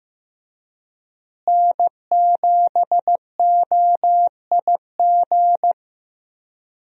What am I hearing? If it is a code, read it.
Morse code, heard as a keyed tone, about 15 wpm, 700 Hz. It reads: N7OIG
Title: USA Call Signs - 5 Characters - Repeated 3x - 15wpm